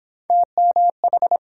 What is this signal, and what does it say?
Morse code, heard as a keyed tone, about 30 wpm, 700 Hz. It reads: TMH